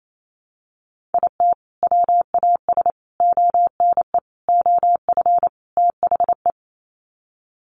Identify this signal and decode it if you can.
Morse code, heard as a keyed tone, about 28 wpm, 700 Hz. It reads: IT WAS ONE OF THE
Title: Sets of 5 Words 28wpm